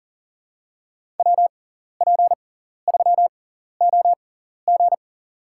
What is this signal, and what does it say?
Morse code, heard as a keyed tone, about 40 wpm, spaced as for 14 wpm, 700 Hz. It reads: WP3OG
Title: USA Call Signs - 6x Character Spacing - 40wpm